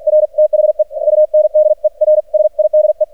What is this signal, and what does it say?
Morse code, heard as a keyed tone, about 44 wpm, 610 Hz. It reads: R THE UNDERSIDE
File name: archived / CRSnet